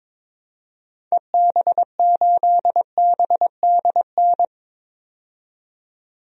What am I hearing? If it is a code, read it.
Morse code, heard as a keyed tone, about 21 wpm, 700 Hz. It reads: EB8BDN
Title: DXCC-100 Call Signs - 6 Characters - 22wpm